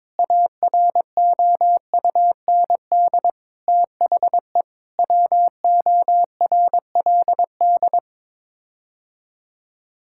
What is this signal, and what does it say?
Morse code, heard as a keyed tone, about 22 wpm, 700 Hz. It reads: AROUND THE WORLD
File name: practice / 22wpm / Sets of 7 Words - 22wpm